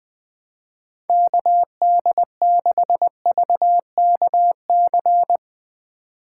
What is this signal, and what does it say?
Morse code, heard as a keyed tone, about 20 wpm, 700 Hz. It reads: KD6VKC
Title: Speed Racing - USA 2x3 Call Signs 17wpm